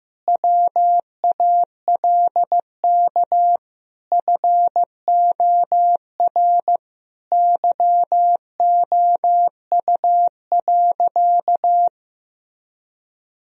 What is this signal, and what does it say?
Morse code, heard as a keyed tone, about 15 wpm, 700 Hz. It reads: WALK FOR YOU.
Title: Sentences from Top 1000 Words - Review Part 1 - 15wpm